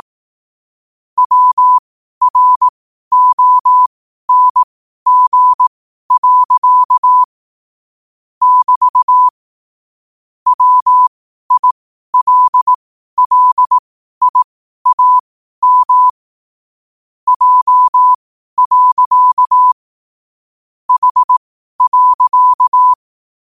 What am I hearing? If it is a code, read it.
Morse code, heard as a keyed tone, about 18 wpm, spaced as for 12 wpm, 1000 Hz. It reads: WRONG. <BT> WILLIAM J. H.
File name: qotdfiles / 12wpm